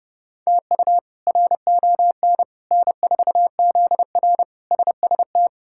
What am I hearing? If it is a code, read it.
Morse code, heard as a keyed tone, about 32 wpm, 700 Hz. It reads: TU RON N4ZR SST
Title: K1USN SST QSO Practice - 30wpm